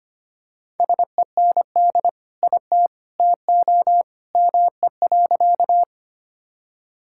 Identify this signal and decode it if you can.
Morse code, heard as a keyed tone, about 25 wpm, 700 Hz. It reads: SEND IT TO ME.
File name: practice / 25wpm / Sentences with 2-Word Phrases - 25wpm